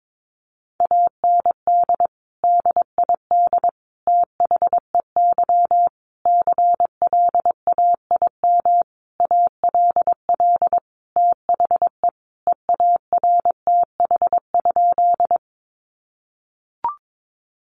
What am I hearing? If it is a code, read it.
Morse code, heard as a keyed tone, about 22 wpm, 700 Hz. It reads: AND DID THEY CLAIM ALL THE EARTH?
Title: Sentences from Top 1500 Words 22wpm